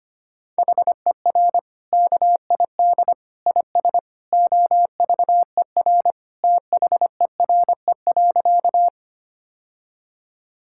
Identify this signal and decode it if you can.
Morse code, heard as a keyed tone, about 25 wpm, 700 Hz. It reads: HER KID IS OVER THERE.
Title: Sentences from Top 1600 Words 25wpm